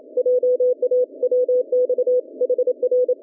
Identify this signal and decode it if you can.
Morse code, heard as a keyed tone, about 27 wpm, 440 Hz. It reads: JA WX HR